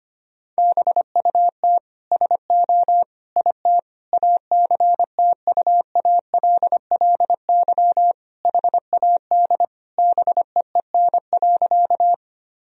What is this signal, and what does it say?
Morse code, heard as a keyed tone, about 25 wpm, 690 Hz. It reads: BUT SO IT ACTUALLY HAD BEEN.
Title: Sentences from Top 1000 Words - Review Part 4 - 25wpm